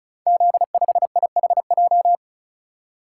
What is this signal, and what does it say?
Morse code, heard as a keyed tone, about 36 wpm, 690 Hz. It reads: Z5IHJ
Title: Code Groups - 5 Letter:Numbers - 35wpm